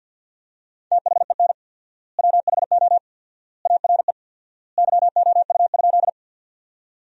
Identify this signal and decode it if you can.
Morse code, heard as a keyed tone, about 50 wpm, 700 Hz. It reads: THEN WHO ARE YOU?
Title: 4x Word-Spacing - Sentences from Top 100 Words - 50wpm